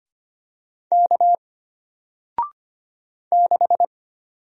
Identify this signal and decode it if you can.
Morse code, heard as a keyed tone, about 25 wpm, 700 Hz. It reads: K 6
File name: Single-Letter-Number-25wpm